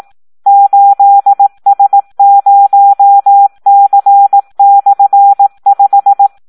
Morse code, heard as a keyed tone, about 18 wpm, 800 Hz. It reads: 8S0C/5
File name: Audio practica CW / SD 1 18